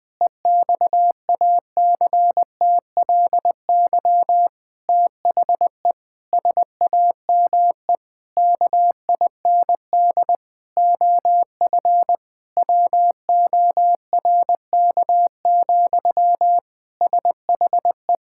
Morse code, heard as a keyed tone, about 21 wpm, 700 Hz. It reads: EXACTLY THE SAME KIND OF WORK, SHE
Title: Sentences from Top 1000 Words - Review Part 6 - 20wpm